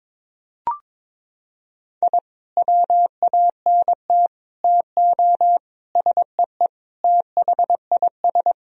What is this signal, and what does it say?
Morse code, heard as a keyed tone, about 22 wpm, 700 Hz. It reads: I WANT TO SEE THIS
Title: Sentences from Top 1000 Words - Review Part 3 - 22wpm